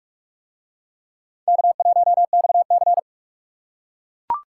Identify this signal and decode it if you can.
Morse code, heard as a keyed tone, about 42 wpm, 700 Hz. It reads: K1XC